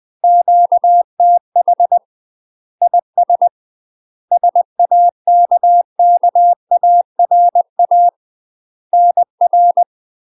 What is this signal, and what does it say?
Morse code, heard as a keyed tone, about 20 wpm, 700 Hz. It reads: QTH IS SAKKARA NR
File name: UK MASTER QSOs TRAINING / 20WPM / UK MASTER QSOs 0042